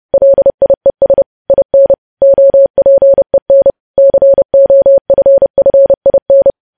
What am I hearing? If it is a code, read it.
Morse code, heard as a keyed tone, about 30 wpm, 550 Hz. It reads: LIES IN OPEN COFFIN
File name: podcast / News headlines-1x-30x30